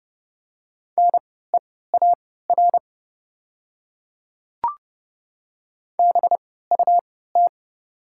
Morse code, heard as a keyed tone, about 30 wpm, spaced as for 17 wpm, 690 Hz. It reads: NEAR BUT
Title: 3x Character Spacing - Top 200 Words - 30wpm